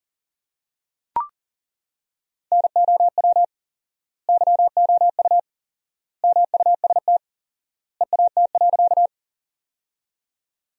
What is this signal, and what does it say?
Morse code, heard as a keyed tone, about 40 wpm, 700 Hz. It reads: NOW YOU MUST EAT.
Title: 4x Word-Spacing - Sentences from Top 300 Words - 40wpm